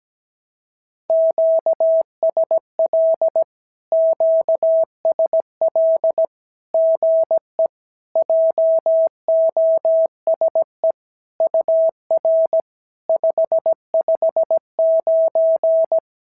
Morse code, heard as a keyed tone, about 17 wpm, 650 Hz. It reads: QSL QSL GE JOSE UR 559
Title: SOTA QSO Practice 17wpm